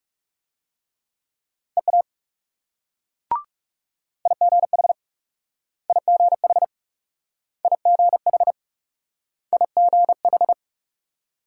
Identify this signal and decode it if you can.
Morse code, heard as a keyed tone, about 42 wpm, 700 Hz. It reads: EA IGH IGH IGH IGH